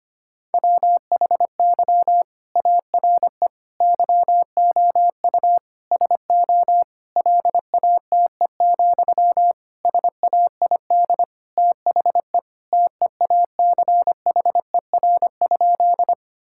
Morse code, heard as a keyed tone, about 25 wpm, 700 Hz. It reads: WHY ARE YOU SO LATE, SAID THE TEACHER?
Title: Sentences from Top 1000 Words - Review Part 6 - 25wpm